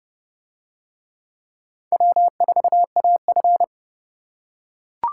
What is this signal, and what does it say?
Morse code, heard as a keyed tone, about 28 wpm, 700 Hz. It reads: W4AF